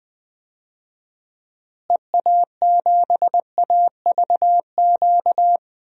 Morse code, heard as a keyed tone, about 20 wpm, 700 Hz. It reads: EA7AVQ
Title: DXCC-100 Call Signs - 6 Characters - Repeated 3x - 20wpm